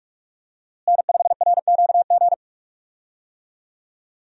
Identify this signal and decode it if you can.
Morse code, heard as a keyed tone, about 46 wpm, 700 Hz. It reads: N5RQG
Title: USA Call Signs - 5 Characters - Repeated 3x - 45wpm